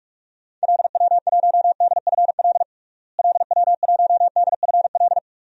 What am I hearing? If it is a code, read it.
Morse code, heard as a keyed tone, about 46 wpm, 700 Hz. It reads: LW1DFL LW1DFL
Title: DXCC-100 Call Signs - 6 Characters - Repeated 3x - 45wpm